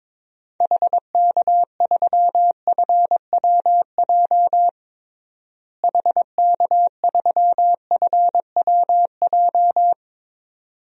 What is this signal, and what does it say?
Morse code, heard as a keyed tone, about 22 wpm, 700 Hz. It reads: HK3FWJ HK3FWJ
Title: DXCC-100 Call Signs - 6 Characters - Repeated 2x - 22wpm